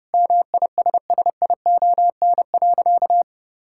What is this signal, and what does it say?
Morse code, heard as a keyed tone, about 30 wpm, 700 Hz. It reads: MISSION.
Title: Sentences from Top 1600 Words 30wpm